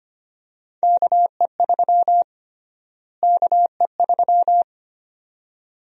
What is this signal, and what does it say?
Morse code, heard as a keyed tone, about 25 wpm, 700 Hz. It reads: KE3 KE3
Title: USA Call Sign Prefixes - Repeated 2x - 25wpm